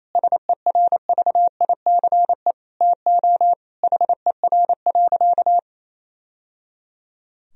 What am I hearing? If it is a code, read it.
Morse code, heard as a keyed tone, about 28 wpm, 700 Hz. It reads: SERVICE TO HER.
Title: Sentences from Top 1000 Words - Review Part 4 - 28wpm